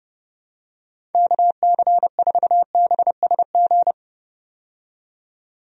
Morse code, heard as a keyed tone, about 32 wpm, 700 Hz. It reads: KC4BSG
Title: USA Call Signs - 6 Characters - 30wpm